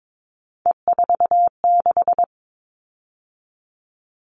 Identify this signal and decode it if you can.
Morse code, heard as a keyed tone, about 22 wpm, 700 Hz. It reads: E46
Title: All Cut Numbers - 3-Digit 22wpm